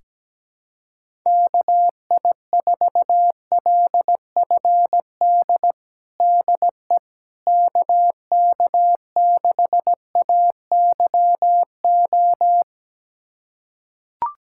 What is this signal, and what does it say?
Morse code, heard as a keyed tone, about 17 wpm, 700 Hz. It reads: KI4LFD DE KK6AYO
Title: USA Callsign DE Callsign 17wpm